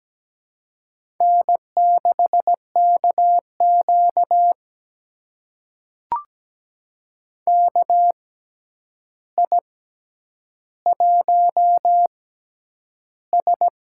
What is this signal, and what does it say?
Morse code, heard as a keyed tone, about 17 wpm, spaced as for 9 wpm, 700 Hz. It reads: N6KQ KI1S
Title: USA Call Signs - 4 Characters - 6x Character Spacing - 17wpm